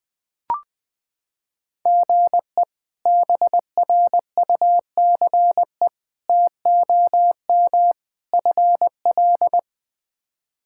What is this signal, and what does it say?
Morse code, heard as a keyed tone, about 20 wpm, 700 Hz. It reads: GE BRUCE TOM FL
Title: K1USN SST QSO Practice - 20wpm